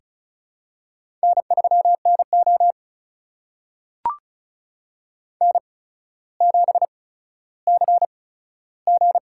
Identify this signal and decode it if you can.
Morse code, heard as a keyed tone, about 36 wpm, spaced as for 15 wpm, 700 Hz. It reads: N3NO N7CG
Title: USA Call Signs - 4 Characters - 8x Character Spacing - 35wpm